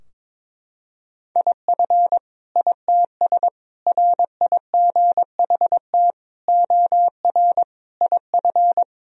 Morse code, heard as a keyed tone, about 22 wpm, 700 Hz. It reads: IF ITS RIGHT OR IF
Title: Sentences from Top 1000 Words - Review Part 3 - 22wpm